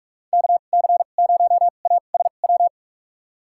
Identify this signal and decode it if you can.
Morse code, heard as a keyed tone, about 46 wpm, spaced as for 33 wpm, 700 Hz. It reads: KC0ASW
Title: USA Call Signs - 2x Character Spacing - 45wpm